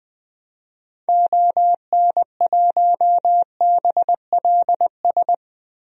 Morse code, heard as a keyed tone, about 20 wpm, 700 Hz. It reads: ON1BLS
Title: DXCC-100 Call Signs - 6 Characters - 20wpm